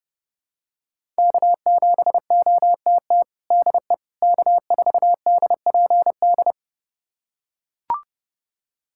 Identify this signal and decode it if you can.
Morse code, heard as a keyed tone, about 30 wpm, 700 Hz. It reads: K7OTT DE K4DPD
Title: USA Callsign DE Callsign - 5 Characters - 30wpm